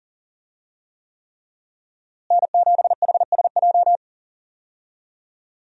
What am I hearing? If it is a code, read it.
Morse code, heard as a keyed tone, about 40 wpm, 700 Hz. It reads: N7HSJ
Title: USA Call Signs - 5 Characters - Repeated 3x - 40wpm